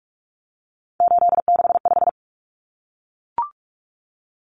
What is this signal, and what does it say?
Morse code, heard as a keyed tone, about 44 wpm, 700 Hz. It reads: 865
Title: Common Cut Numbers - 3-Digit 45wpm